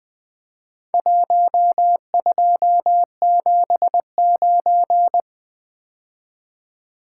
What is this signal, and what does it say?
Morse code, heard as a keyed tone, about 20 wpm, 700 Hz. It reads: 1279
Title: Four Letter-Number Combinations - 20wpm